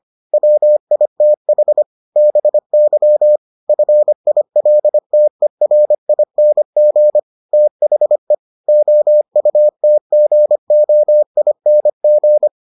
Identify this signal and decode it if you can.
Morse code, heard as a keyed tone, about 25 wpm, 600 Hz. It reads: WITH BY FILTERING THE OUTGOING